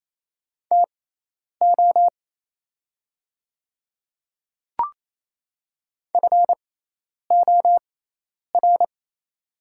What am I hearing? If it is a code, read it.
Morse code, heard as a keyed tone, about 28 wpm, spaced as for 10 wpm, 700 Hz. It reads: TO FOR
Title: 6x Character Spacing - Top 200 Words - 28wpm